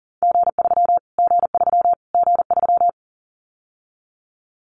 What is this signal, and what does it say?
Morse code, heard as a keyed tone, about 37 wpm, 700 Hz. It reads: G3 G3 G3
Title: Callsign Prefixes for Top 25 DXCC - Repeated 3x - 40wpm